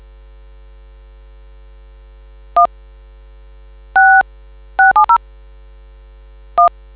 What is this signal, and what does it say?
DTMF, heard as telephone touch tones, tones of about 120 ms. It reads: 1 6670 1